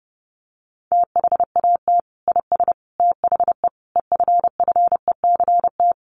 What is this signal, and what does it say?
Morse code, heard as a keyed tone, about 30 wpm, 700 Hz. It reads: THAT IS THE EFFECT
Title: Sentences from Top 1300 Words 30wpm